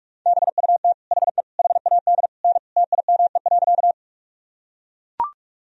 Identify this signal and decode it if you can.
Morse code, heard as a keyed tone, about 44 wpm, 700 Hz. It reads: BUT HE HAD N TIME.
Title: Sentences from Top 100 Words and Abbreviations 45wpm